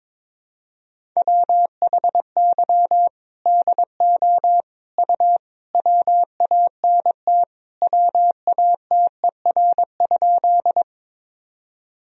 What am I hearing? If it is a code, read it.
Morse code, heard as a keyed tone, about 22 wpm, 700 Hz. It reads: WHY DO U WANT WATER?
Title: Sentences from Top 200 Words and Abbreviations 22wpm